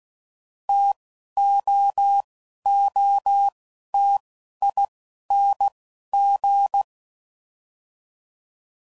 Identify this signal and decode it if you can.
Morse code, heard as a keyed tone, about 16 wpm, spaced as for 11 wpm, 800 Hz. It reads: TOOTING
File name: Combinations / Difficult Combos / T AOEI's